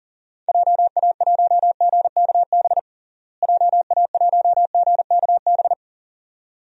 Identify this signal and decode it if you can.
Morse code, heard as a keyed tone, about 37 wpm, 700 Hz. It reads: JA1GKB JA1GKB
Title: DXCC-100 Call Signs - 6 Characters - Repeated 2x - 40wpm